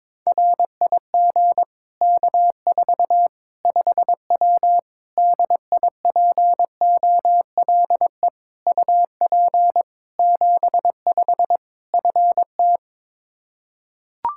Morse code, heard as a keyed tone, about 22 wpm, 700 Hz. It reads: RIG K4 5W DIPOLE UP 75 FT
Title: Practice QSO - Exchange 2 - Info - 22wpm